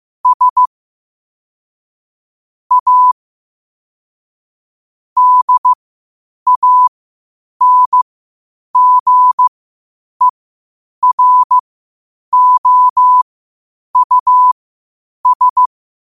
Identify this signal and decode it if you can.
Morse code, heard as a keyed tone, about 15 wpm, spaced as for 8 wpm, 1000 Hz. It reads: S A DANGEROUS
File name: qotdfiles / 8wpm